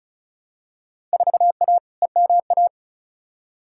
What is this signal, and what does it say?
Morse code, heard as a keyed tone, about 36 wpm, 700 Hz. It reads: 4A EMA
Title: Field Day Exchanges - 35wpm